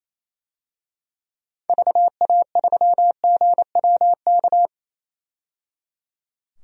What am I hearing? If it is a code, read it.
Morse code, heard as a keyed tone, about 28 wpm, 700 Hz. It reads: VA3GWK
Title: DXCC-100 Call Signs - 6 Characters - 28wpm